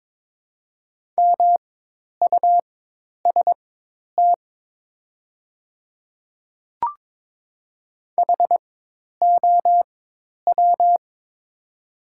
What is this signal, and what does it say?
Morse code, heard as a keyed tone, about 22 wpm, spaced as for 10 wpm, 700 Hz. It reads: MUST HOW
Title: 4x Character Spacing - Top 200 Words - 22wpm